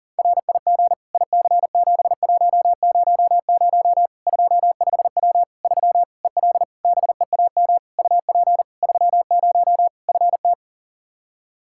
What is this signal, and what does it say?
Morse code, heard as a keyed tone, about 40 wpm, 700 Hz. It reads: RIG IC7100 25W 3 EL BEAM UP 30 FT
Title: Practice QSO - Exchange 2 - Info - 40wpm